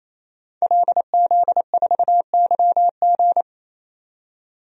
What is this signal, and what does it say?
Morse code, heard as a keyed tone, about 28 wpm, 700 Hz. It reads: LZ4YG
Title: DXCC-100 Call Signs - 5 Characters - 28wpm